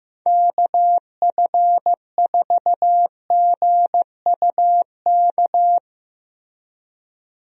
Morse code, heard as a keyed tone, about 15 wpm, 700 Hz. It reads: KF4GUK
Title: USA Call Signs - 6 Characters - Repeated 2x - 15wpm